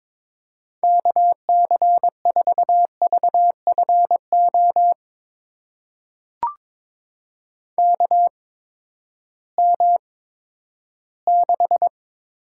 Morse code, heard as a keyed tone, about 22 wpm, spaced as for 12 wpm, 700 Hz. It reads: KC4VFO KM6
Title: USA Call Signs - 8x Character Spacing - 22wpm